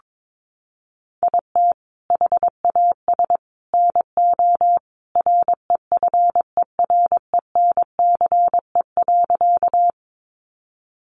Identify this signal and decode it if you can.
Morse code, heard as a keyed tone, about 22 wpm, 700 Hz. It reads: IT HAS NO REFERENCE.
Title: Sentences from Top 1200 Words 22wpm